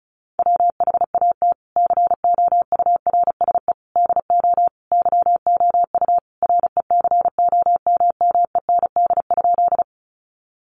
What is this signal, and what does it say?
Morse code, heard as a keyed tone, about 34 wpm, 700 Hz. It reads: WHAT COURSE DO YOU RECOMMEND?
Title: Sentences from Top 1200 Words 35wpm